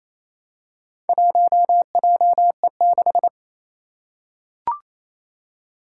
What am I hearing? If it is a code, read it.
Morse code, heard as a keyed tone, about 28 wpm, 700 Hz. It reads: 1JE6
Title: Four Challenging Letter-Number Combinations 28wpm